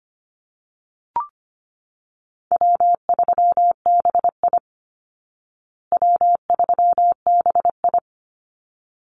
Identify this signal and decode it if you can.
Morse code, heard as a keyed tone, about 25 wpm, 700 Hz. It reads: W3BI W3BI
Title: Easy Call Signs Repeated 2x - 25wpm